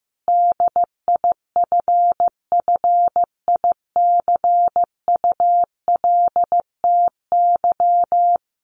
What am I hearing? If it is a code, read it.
Morse code, heard as a keyed tone, about 15 wpm, 700 Hz. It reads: DIFFICULTY
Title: Sentences from Top 1500 Words 15wpm